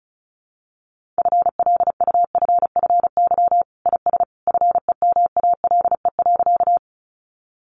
Morse code, heard as a keyed tone, about 34 wpm, 700 Hz. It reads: FLUFFY IS FEMALE.
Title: Easy Phrases 35wpm